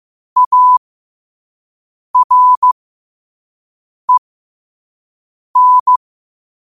Morse code, heard as a keyed tone, about 15 wpm, spaced as for 5 wpm, 1000 Hz. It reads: AREN
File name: qotdfiles / 5wpm